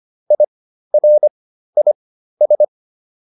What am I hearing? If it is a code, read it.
Morse code, heard as a keyed tone, about 25 wpm, spaced as for 13 wpm, 600 Hz. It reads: IRIS